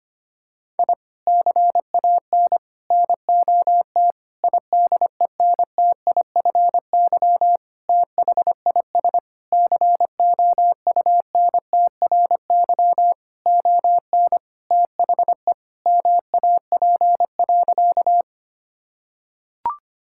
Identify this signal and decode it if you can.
Morse code, heard as a keyed tone, about 25 wpm, 700 Hz. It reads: I CAN NOT IDENTIFY THIS COUNTRY ON THE MAP.
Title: Sentences from Top 1000 Words - Review Part 6 - 25wpm